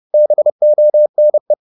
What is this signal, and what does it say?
Morse code, heard as a keyed tone, about 30 wpm, 600 Hz. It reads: BONE